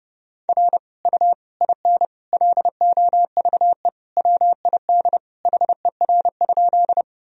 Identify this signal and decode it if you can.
Morse code, heard as a keyed tone, about 30 wpm, 700 Hz. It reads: R U IN LOVE WID HER?
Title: Sentences from Top 400 Words and Abbreviations 30wpm